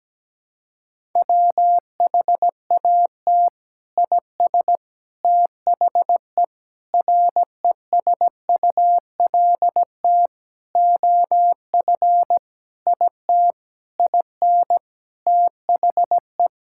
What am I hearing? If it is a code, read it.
Morse code, heard as a keyed tone, about 17 wpm, 700 Hz. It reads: WHAT IS THE RESULT OF IT IN THE
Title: Sentences from Top 1000 Words - Review Part 5 - 17wpm